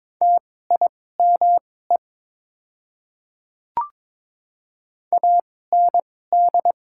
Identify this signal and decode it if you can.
Morse code, heard as a keyed tone, about 22 wpm, spaced as for 16 wpm, 690 Hz. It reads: TIME AND
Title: 2x Character Spacing - Top 100 Words - 22wpm